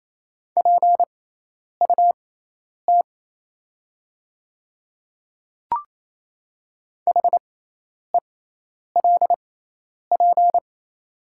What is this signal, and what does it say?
Morse code, heard as a keyed tone, about 27 wpm, spaced as for 10 wpm, 690 Hz. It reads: PUT HELP
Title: 6x Character Spacing - Top 300 Words - 28wpm